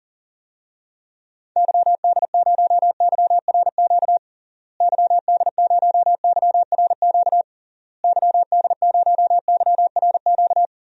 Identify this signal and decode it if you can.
Morse code, heard as a keyed tone, about 40 wpm, 700 Hz. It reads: YD0YRQ YD0YRQ YD0YRQ
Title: DXCC-100 Call Signs - 6 Characters - Repeated 3x - 40wpm